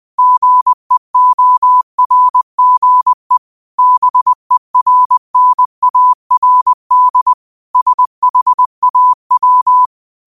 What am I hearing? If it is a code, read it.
Morse code, heard as a keyed tone, about 20 wpm, 1000 Hz. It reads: GEORGE BERNARD SHAW